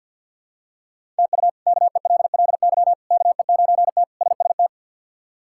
Morse code, heard as a keyed tone, about 46 wpm, 690 Hz. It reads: TU KELLY KE9T SST
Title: K1USN SST QSO Practice - 50wpm